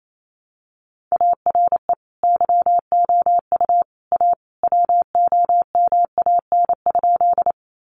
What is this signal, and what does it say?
Morse code, heard as a keyed tone, about 28 wpm, 700 Hz. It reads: ARE YOU A WOMAN?
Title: Sentences from Top 1100 Words 28wpm